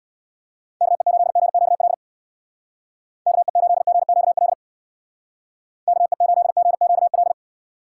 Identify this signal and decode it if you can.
Morse code, heard as a keyed tone, about 59 wpm, 700 Hz. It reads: KE8GQL KE8GQL KE8GQL
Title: Speed Racing - USA Call Signs 50wpm